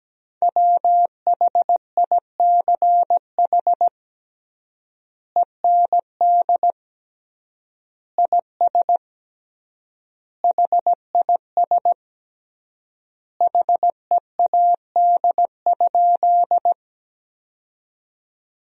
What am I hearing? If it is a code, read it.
Morse code, heard as a keyed tone, about 17 wpm, 700 Hz. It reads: WHICH END IS HIS HEAD?
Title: 3x Word-Spacing - Sentences from Top 300 Words - 17wpm